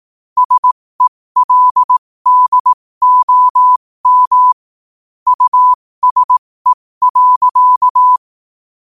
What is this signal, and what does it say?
Morse code, heard as a keyed tone, about 18 wpm, spaced as for 15 wpm, 1000 Hz. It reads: SELDOM USE.
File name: qotdfiles / 15wpm